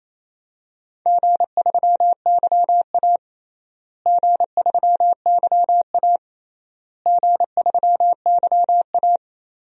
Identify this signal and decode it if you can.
Morse code, heard as a keyed tone, about 27 wpm, 700 Hz. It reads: G3YA G3YA G3YA
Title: DXCC-100 Call Signs - 4 Characters - Repeated 3x - 28wpm